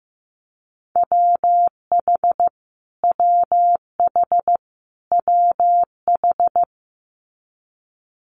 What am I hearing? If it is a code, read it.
Morse code, heard as a keyed tone, about 15 wpm, 700 Hz. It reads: WH WH WH
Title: Multi-letter Phonemes Repeated 3x 15wpm